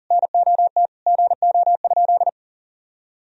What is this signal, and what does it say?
Morse code, heard as a keyed tone, about 40 wpm, 700 Hz. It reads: NOT GO?
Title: Sentences from Top 1000 Words - Review Part 4 - 40wpm